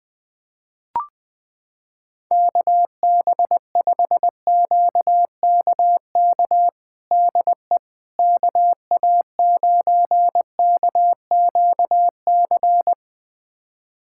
Morse code, heard as a keyed tone, about 20 wpm, 700 Hz. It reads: KB5QKK DE KA9KQC
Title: USA Callsign DE Callsign - 6 Characters - 20wpm